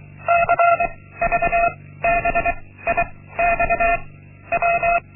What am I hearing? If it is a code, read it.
Morse code, heard as a keyed tone, about 24 wpm, 690 Hz. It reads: C V B I X W
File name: Linksmakalnis-1